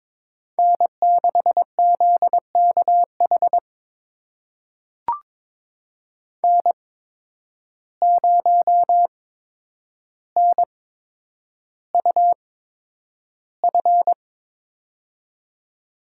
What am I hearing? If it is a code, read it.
Morse code, heard as a keyed tone, about 22 wpm, spaced as for 9 wpm, 700 Hz. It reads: N6ZKH N0NUF
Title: USA Call Signs - 5 Characters - 8x Character Spacing - 22wpm